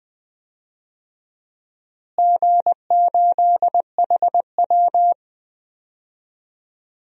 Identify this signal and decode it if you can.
Morse code, heard as a keyed tone, about 20 wpm, 700 Hz. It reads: G8HW
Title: DXCC-100 Call Signs - 4 Characters - Repeated 2x - 20wpm